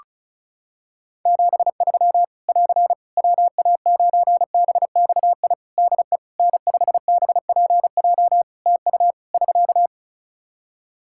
Morse code, heard as a keyed tone, about 36 wpm, 700 Hz. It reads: 73 <AR> WA9BXI DE N5BPJ TU <SK>
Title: K1USN SST QSO Practice 35wpm